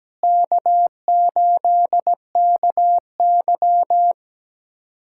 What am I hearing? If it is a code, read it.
Morse code, heard as a keyed tone, about 17 wpm, 700 Hz. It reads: K8KY
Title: USA Callsign DE Callsign 17wpm